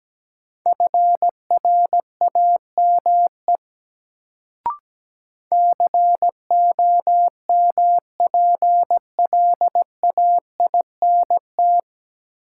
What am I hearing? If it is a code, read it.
Morse code, heard as a keyed tone, about 17 wpm, 700 Hz. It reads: FRAME COMPLAINT
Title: Sentences from Top 1400 Words - New Word Drill Alt - 17wpm